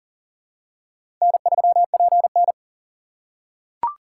Morse code, heard as a keyed tone, about 40 wpm, 700 Hz. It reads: N3PN
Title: USA Call Signs - 4 Characters - 3x Character Spacing - 40wpm